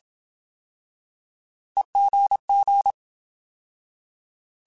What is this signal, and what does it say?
Morse code, heard as a keyed tone, about 27 wpm, 800 Hz. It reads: EGG
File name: Alphabetical / Alphabetical mp3s / Es @25